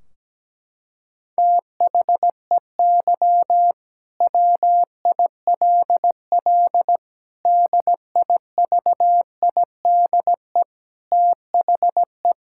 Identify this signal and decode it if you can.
Morse code, heard as a keyed tone, about 17 wpm, 700 Hz. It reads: THEY WILL DIVIDE THE
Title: Sentences from Top 1600 Words 17wpm